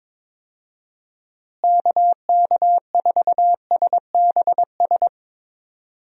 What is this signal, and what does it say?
Morse code, heard as a keyed tone, about 22 wpm, 700 Hz. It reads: KK4SBS
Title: USA Call Signs - 6 Characters - Repeated 3x - 22wpm